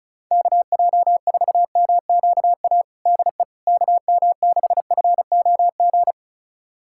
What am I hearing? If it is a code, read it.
Morse code, heard as a keyed tone, about 36 wpm, 700 Hz. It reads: KJ4MQA DE KM6FOG
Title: USA Callsign DE Callsign - 6 Characters - 35wpm